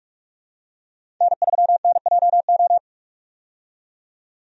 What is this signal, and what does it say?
Morse code, heard as a keyed tone, about 45 wpm, 700 Hz. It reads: N3NJO
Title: USA Call Signs - 5 Characters - 45wpm